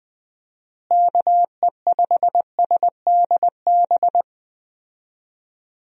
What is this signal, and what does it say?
Morse code, heard as a keyed tone, about 21 wpm, 700 Hz. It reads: KE5SDB
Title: USA Call Signs - 6 Characters - Repeated 2x - 20wpm